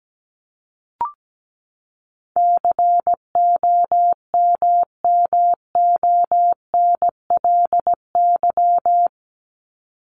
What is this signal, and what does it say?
Morse code, heard as a keyed tone, about 17 wpm, 700 Hz. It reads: COMMONLY
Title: LY Suffix - Top 100 Words with Base Words 17wpm